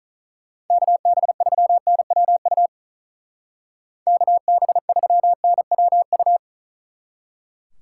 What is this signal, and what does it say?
Morse code, heard as a keyed tone, about 43 wpm, 700 Hz. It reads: KB3NWU KB3NWU
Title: Speed Racing - USA 2x3 Call Signs 35wpm